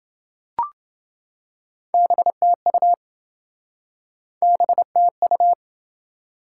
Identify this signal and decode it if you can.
Morse code, heard as a keyed tone, about 28 wpm, 700 Hz. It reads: BTU BTU